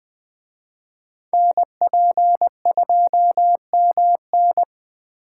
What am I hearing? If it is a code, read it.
Morse code, heard as a keyed tone, about 20 wpm, 700 Hz. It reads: NP2MN
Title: Speed Racing - USA Call Signs 17wpm